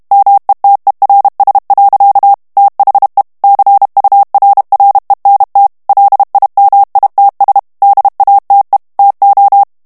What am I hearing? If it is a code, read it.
Morse code, heard as a keyed tone, about 32 wpm, 800 Hz. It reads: METERS. THE CURRENT LIMITS DATE TO
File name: morse mp3 / ANGLAIS texte clair 3 vitesse 30 mots minute